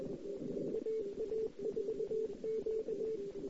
Morse code, heard as a keyed tone, about 21 wpm, 440 Hz, with noise. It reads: AA4Q
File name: CW / 2008-ss-cw-k5tr-op-n5rz -06:00